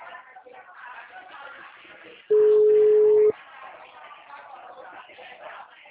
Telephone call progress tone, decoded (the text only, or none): ringback tone